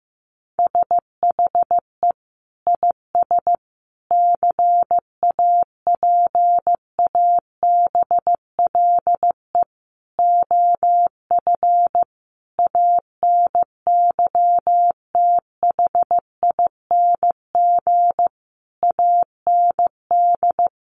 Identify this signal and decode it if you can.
Morse code, heard as a keyed tone, about 15 wpm, 700 Hz. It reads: SHE IS CAPABLE OF ANYTHING AND
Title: Sentences from Top 1200 Words 15wpm